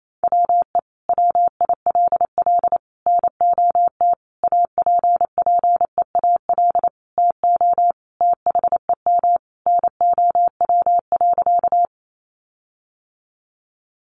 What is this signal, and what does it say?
Morse code, heard as a keyed tone, about 28 wpm, 700 Hz. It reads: WE WILL NOT APPEAL TO THEM NOW.
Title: Sentences from Top 1300 Words 28wpm